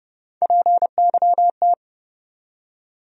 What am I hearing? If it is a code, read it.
Morse code, heard as a keyed tone, about 29 wpm, 700 Hz. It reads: PYT
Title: USA Call Sign Postfixes - 3 Letters 30wpm